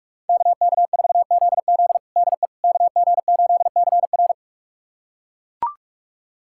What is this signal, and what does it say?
Morse code, heard as a keyed tone, about 41 wpm, 700 Hz. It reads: KK4ZZ DE KG8CR
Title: USA Callsign DE Callsign - 5 Characters - 45wpm